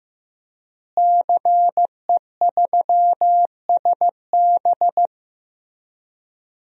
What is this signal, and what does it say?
Morse code, heard as a keyed tone, about 15 wpm, 700 Hz. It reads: CE3SB
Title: DXCC-100 Call Signs - 5 Characters - Repeated 2x - 15wpm